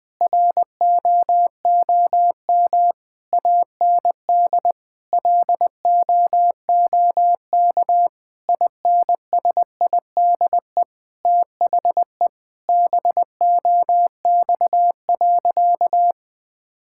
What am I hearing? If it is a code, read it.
Morse code, heard as a keyed tone, about 20 wpm, 700 Hz. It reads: ROOM AND LOOK INSIDE THE BOX.
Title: Sentences from Top 1000 Words - Review Part 6 - 20wpm